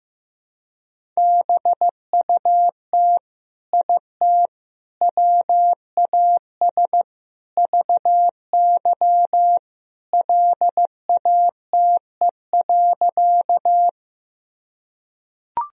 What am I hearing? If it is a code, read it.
Morse code, heard as a keyed tone, about 15 wpm, 700 Hz. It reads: BUT IT WAS VY LATE.
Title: Sentences from Top 300 Words and Abbreviations 15wpm